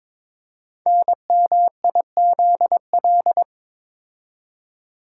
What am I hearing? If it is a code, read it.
Morse code, heard as a keyed tone, about 22 wpm, 700 Hz. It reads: NMIZL
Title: Code Groups - 5 Letters - 22wpm